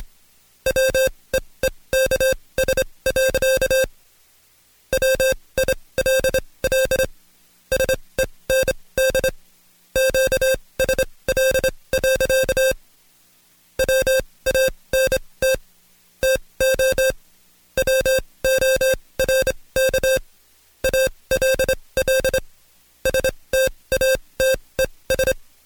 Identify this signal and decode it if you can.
Morse code, heard as a keyed tone, about 26 wpm, spaced as for 20 wpm, 530 Hz. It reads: WEEKS. WILL SEND QSL. WANT TO WORK ALL STATES
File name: Code Quick / AudioDownld2 / Seg136-Stepper9 19WPM